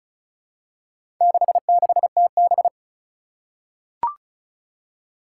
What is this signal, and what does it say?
Morse code, heard as a keyed tone, about 36 wpm, 700 Hz. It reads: 66TB
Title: Four Challenging Letter-Number Combinations 35wpm